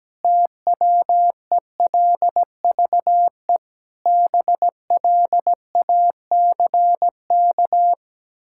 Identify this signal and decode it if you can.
Morse code, heard as a keyed tone, about 17 wpm, 700 Hz. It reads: TWELVE BLACK
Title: Pangram Challenge v2 - 17wpm